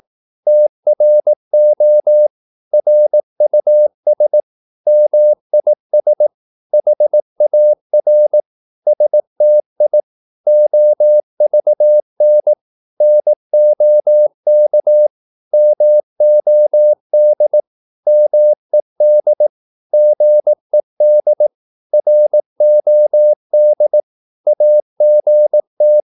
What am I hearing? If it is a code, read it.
Morse code, heard as a keyed tone, about 18 wpm, 600 Hz. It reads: TRO RUS MIS HAR STI OVN NOK MOD MED GED ROD AGT
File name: Korte ord DK 18wpm